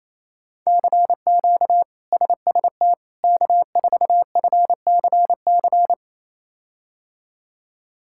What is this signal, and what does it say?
Morse code, heard as a keyed tone, about 28 wpm, 700 Hz. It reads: CQ SST K4FCC
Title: K1USN SST QSO Practice - 28wpm